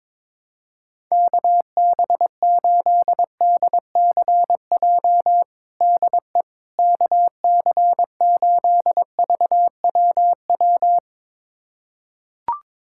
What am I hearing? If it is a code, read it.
Morse code, heard as a keyed tone, about 22 wpm, 700 Hz. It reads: KB8DCJ DE KC8VWW
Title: USA Callsign DE Callsign - 6 Characters - 22wpm